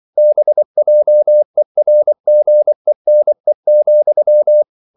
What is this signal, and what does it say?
Morse code, heard as a keyed tone, about 24 wpm, 600 Hz. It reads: BJERGENE,
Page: Ferie 24 wpm